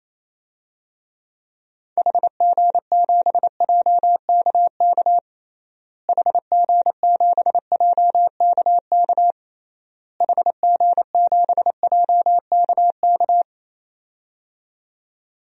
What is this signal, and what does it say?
Morse code, heard as a keyed tone, about 28 wpm, 700 Hz. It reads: HG7JKK HG7JKK HG7JKK
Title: DXCC-100 Call Signs - 6 Characters - Repeated 3x - 28wpm